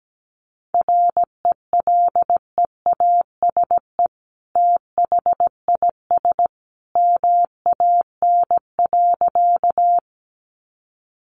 Morse code, heard as a keyed tone, about 17 wpm, 700 Hz. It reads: RELEASE THIS MAN.
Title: Sentences from Top 1100 Words 17wpm